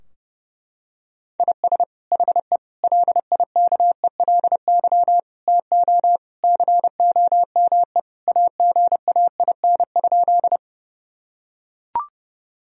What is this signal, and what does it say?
Morse code, heard as a keyed tone, about 32 wpm, 700 Hz. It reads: IS HE LIKELY TO COME AGAIN?
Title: Sentences from Top 1000 Words - Review Part 4 - 30wpm